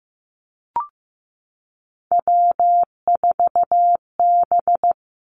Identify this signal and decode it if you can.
Morse code, heard as a keyed tone, about 15 wpm, 700 Hz. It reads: W4B
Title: Three Challenging Letter-Number Combinations 15wpm